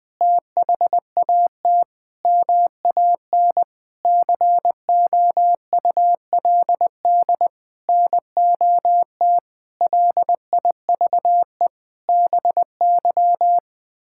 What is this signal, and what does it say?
Morse code, heard as a keyed tone, about 20 wpm, 700 Hz. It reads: THAT MAN COULD NOT LIVE BY
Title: Sentences from Top 1600 Words 20wpm